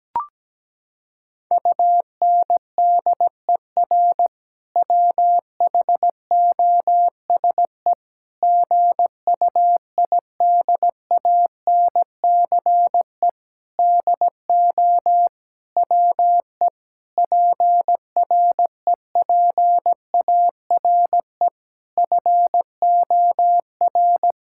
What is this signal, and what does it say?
Morse code, heard as a keyed tone, about 17 wpm, 700 Hz. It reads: UNDER WHOSE GUIDANCE DO WE PREPARE FOR
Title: Sentences from Top 1600 Words 17wpm